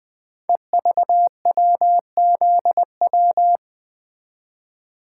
Morse code, heard as a keyed tone, about 20 wpm, 700 Hz. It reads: EVWZW
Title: Code Groups - 5 Letter:Numbers - 20wpm